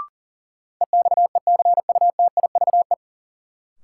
Morse code, heard as a keyed tone, about 37 wpm, 700 Hz. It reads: EXECUTIVE
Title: Sets of 5 Words Encore - New Word Drill - 40wpm